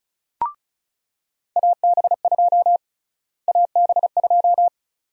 Morse code, heard as a keyed tone, about 36 wpm, 700 Hz. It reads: AB2 AB2
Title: USA Call Sign Prefixes - Repeated 2x - 35wpm